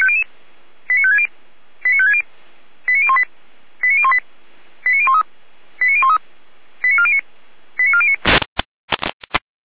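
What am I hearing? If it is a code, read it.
Five-tone selective call (ZVEI), heard as quick runs of five tones, tones of about 70 ms. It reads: 89560 89560 89018 89018 89013 89013 89409 89409
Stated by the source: ZVEI five-tone selective call 89391 received by SDR, from feuerwehr-orlamuende.org (fire brigade Orlamünde).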